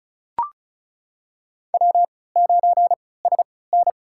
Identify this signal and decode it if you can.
Morse code, heard as a keyed tone, about 36 wpm, spaced as for 20 wpm, 700 Hz. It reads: W9SN
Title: USA Call Signs - 4 Characters - 3x Character Spacing - 35wpm